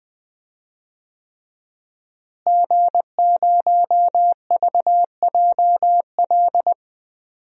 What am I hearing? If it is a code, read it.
Morse code, heard as a keyed tone, about 20 wpm, 700 Hz. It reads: G0VJL
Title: DXCC-100 Call Signs - 5 Characters - Repeated 3x - 20wpm